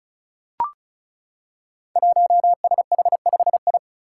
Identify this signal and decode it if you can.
Morse code, heard as a keyed tone, about 36 wpm, 700 Hz. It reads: 1SH5I
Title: Five Challenging Letter-Number Combinations 35wpm